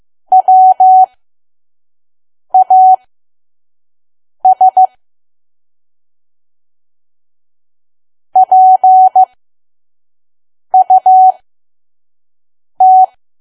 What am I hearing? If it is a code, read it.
Morse code, heard as a keyed tone, about 15 wpm, spaced as for 5 wpm, 750 Hz. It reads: WAS PUT
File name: Archive / 5 WPM files / PM